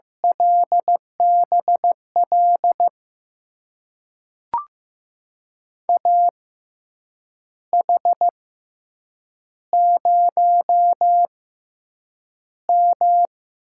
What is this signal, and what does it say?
Morse code, heard as a keyed tone, about 15 wpm, spaced as for 7 wpm, 700 Hz. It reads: LBL AH0M
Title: USA Call Signs - 6x Character Spacing - 15wpm